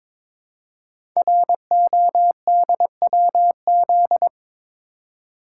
Morse code, heard as a keyed tone, about 22 wpm, 700 Hz. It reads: RODWZ
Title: Code Groups - 5 Letters - 22wpm